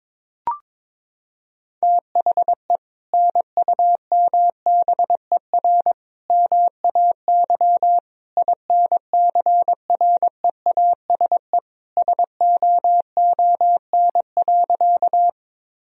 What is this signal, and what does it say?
Morse code, heard as a keyed tone, about 22 wpm, 700 Hz. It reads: THE NUMBER MAY INCREASE SOON.
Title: Sentences from Top 1000 Words - Review Part 4 - 22wpm